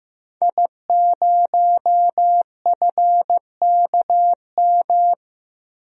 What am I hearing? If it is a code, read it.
Morse code, heard as a keyed tone, about 16 wpm, 700 Hz. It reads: I0FKM
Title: DXCC-100 Call Signs - 5 Characters - Repeated 3x - 15wpm